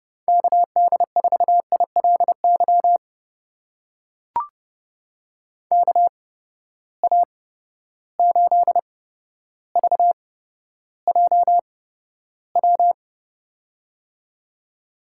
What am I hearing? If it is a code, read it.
Morse code, heard as a keyed tone, about 30 wpm, spaced as for 13 wpm, 700 Hz. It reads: KD4ILY KA8VJW
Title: USA Call Signs - 6 Characters - 8x Character Spacing - 30wpm